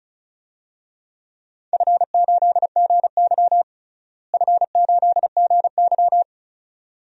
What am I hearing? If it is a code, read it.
Morse code, heard as a keyed tone, about 34 wpm, 700 Hz. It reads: F8GY F8GY
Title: DXCC-100 Call Signs - 4 Characters - Repeated 2x - 35wpm